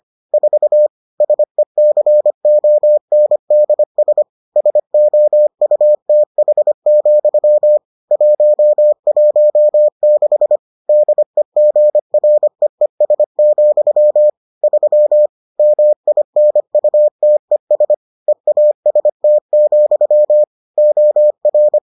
Morse code, heard as a keyed tone, about 25 wpm, 600 Hz. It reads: 4 SECONDS SOUTH, 116 DEGREES, 3 MINUTES EAST, OR